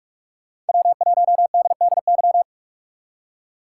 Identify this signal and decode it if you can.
Morse code, heard as a keyed tone, about 59 wpm, 700 Hz. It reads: W1DDY